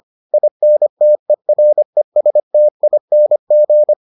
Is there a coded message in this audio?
Yes: Morse code from 0.3 to 3.9 s, heard as a keyed tone, about 25 wpm, 600 Hz. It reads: INTERESTING